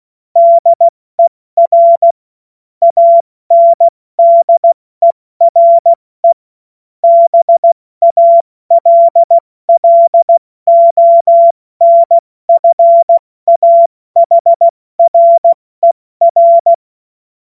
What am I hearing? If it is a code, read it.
Morse code, heard as a keyed tone, about 16 wpm, 680 Hz. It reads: DER ANDERE BALLONFAHRER
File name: Morsekurs / Witze / Witz 03. Mai 2022 70 F80